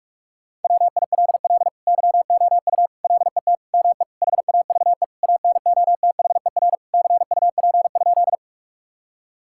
Morse code, heard as a keyed tone, about 46 wpm, 700 Hz. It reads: WILL YOU LET ME HAVE ANOTHER CUP?
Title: Sentences from Top 1600 Words 45wpm